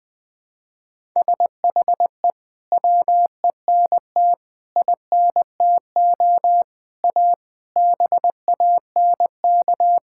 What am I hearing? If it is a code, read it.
Morse code, heard as a keyed tone, about 21 wpm, 700 Hz. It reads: SHE WENT INTO A BANK
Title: Sentences from Top 1000 Words - Review Part 6 - 20wpm